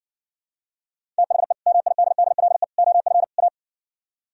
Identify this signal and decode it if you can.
Morse code, heard as a keyed tone, about 68 wpm, 700 Hz. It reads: THE MIDDLE OF A